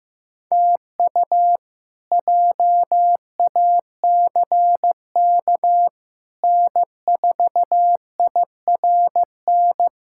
Morse code, heard as a keyed tone, about 16 wpm, 700 Hz. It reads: TU JACK N4IRN
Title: K1USN SST QSO Practice - 15wpm